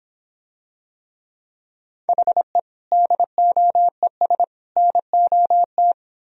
Morse code, heard as a keyed tone, about 26 wpm, 700 Hz. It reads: HE DOES NOT